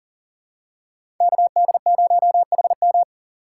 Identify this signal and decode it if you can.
Morse code, heard as a keyed tone, about 38 wpm, 700 Hz. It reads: KD0HM
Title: USA Call Signs - 3x Character Spacing - 40wpm